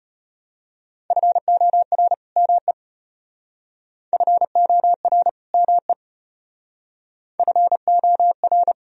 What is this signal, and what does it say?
Morse code, heard as a keyed tone, about 31 wpm, 700 Hz. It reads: FOR ME FOR ME FOR
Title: Speed-Racing - Sets of 2 Words 25wpm